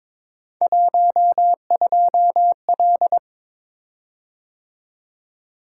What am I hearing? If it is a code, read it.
Morse code, heard as a keyed tone, about 22 wpm, 700 Hz. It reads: 12L
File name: practice / 22wpm / Three Letter-Number Combinations - 22wpm